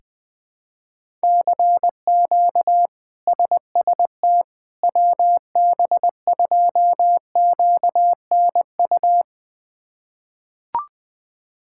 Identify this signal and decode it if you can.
Morse code, heard as a keyed tone, about 20 wpm, 700 Hz. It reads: CQ SST WB2QNU
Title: K1USN SST QSO Practice - 20wpm